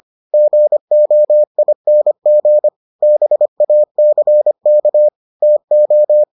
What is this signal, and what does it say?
Morse code, heard as a keyed tone, about 25 wpm, 600 Hz. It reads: GOING BACK TO